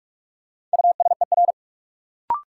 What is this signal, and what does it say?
Morse code, heard as a keyed tone, about 46 wpm, 700 Hz. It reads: USER
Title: Sentences from Top 1000 Words - Review Part 5 - New Word Drill - 45wpm